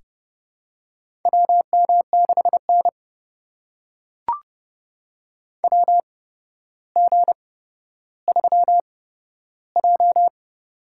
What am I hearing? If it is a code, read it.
Morse code, heard as a keyed tone, about 30 wpm, spaced as for 13 wpm, 700 Hz. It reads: WM6N WG3J
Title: USA Call Signs - 8x Character Spacing - 30wpm